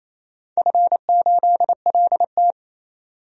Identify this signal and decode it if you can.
Morse code, heard as a keyed tone, about 28 wpm, 700 Hz. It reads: F8LT
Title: DXCC-100 Call Signs - 4 Characters - 28wpm